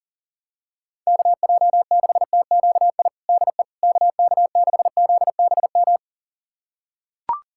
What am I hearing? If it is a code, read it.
Morse code, heard as a keyed tone, about 37 wpm, 700 Hz. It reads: KJ6TQI DE KK6ZBM
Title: USA Callsign DE Callsign 40wpm